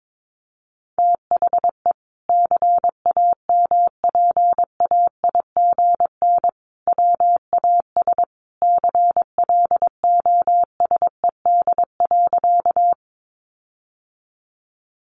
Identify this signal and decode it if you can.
Morse code, heard as a keyed tone, about 22 wpm, 700 Hz. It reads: THE CAMPAIGN WAS CLOSED.
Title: Sentences from Top 1400 Words 22wpm